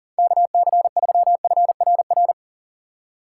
Morse code, heard as a keyed tone, about 41 wpm, 700 Hz. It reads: KC3FRR
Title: USA Call Signs - 3x Character Spacing - 40wpm